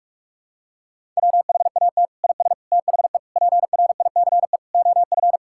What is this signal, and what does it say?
Morse code, heard as a keyed tone, about 46 wpm, 700 Hz. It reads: WHAT IS THE PRICE OF
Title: Sentences from Top 1000 Words - Review Part 5 - 45wpm